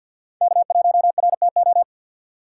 Morse code, heard as a keyed tone, about 52 wpm, 700 Hz. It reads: K1RTO
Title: USA Call Signs - 6x Character Spacing - 50wpm